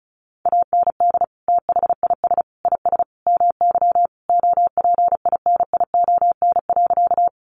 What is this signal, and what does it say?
Morse code, heard as a keyed tone, about 34 wpm, 700 Hz. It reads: AND THIS IS MY OPINION.
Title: Sentences from Top 1200 Words 35wpm